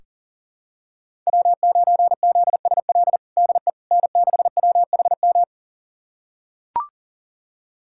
Morse code, heard as a keyed tone, about 40 wpm, 700 Hz. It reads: W9ZSL DE N6WHM
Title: USA Callsign DE Callsign - 5 Characters - 40wpm